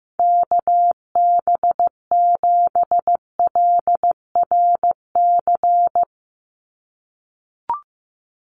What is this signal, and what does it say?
Morse code, heard as a keyed tone, about 15 wpm, 700 Hz. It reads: KB7LRC
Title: USA Call Signs 15wpm